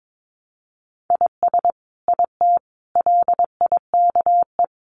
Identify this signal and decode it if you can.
Morse code, heard as a keyed tone, about 22 wpm, 700 Hz. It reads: IS IT LIKE
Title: Sentences from Top 100 Words 22wpm